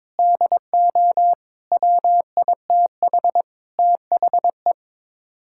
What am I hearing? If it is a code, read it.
Morse code, heard as a keyed tone, about 22 wpm, 700 Hz. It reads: DO WITH THE
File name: practice / 22wpm / Build A Phrase - 5-Word Phrases - 22wpm